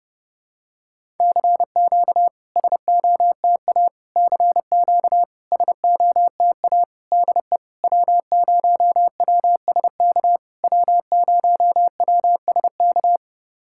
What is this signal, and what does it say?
Morse code, heard as a keyed tone, about 30 wpm, 700 Hz. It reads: CQ SOTA CQ SOTA DE W0WSK W0WSK
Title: SOTA QSO Practice 30wpm